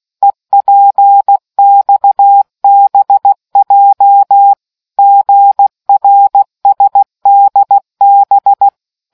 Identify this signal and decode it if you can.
Morse code, heard as a keyed tone, about 16 wpm, 800 Hz. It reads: EPXBJ GRSDB